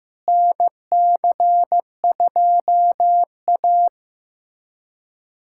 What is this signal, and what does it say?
Morse code, heard as a keyed tone, about 15 wpm, 700 Hz. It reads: NC2A
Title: USA Callsign DE Callsign - 4 Characters - 15wpm